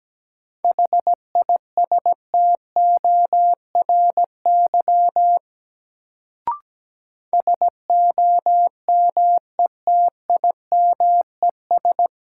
Morse code, heard as a keyed tone, about 17 wpm, 700 Hz. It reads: HISTORY SOMETIMES
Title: Sentences from Top 600 Words - New Word Drill - 17wpm